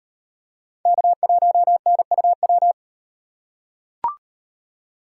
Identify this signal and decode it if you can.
Morse code, heard as a keyed tone, about 38 wpm, 700 Hz. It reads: K1NUW